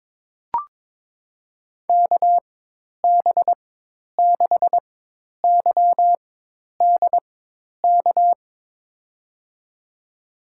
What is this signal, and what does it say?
Morse code, heard as a keyed tone, about 22 wpm, spaced as for 10 wpm, 700 Hz. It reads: KB6YDK
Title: USA Call Signs - 6 Characters - 4x Character Spacing - 22wpm